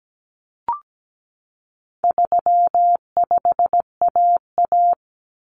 Morse code, heard as a keyed tone, about 17 wpm, 700 Hz. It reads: 35AA
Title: Common Cut Numbers - 4-Digit 17wpm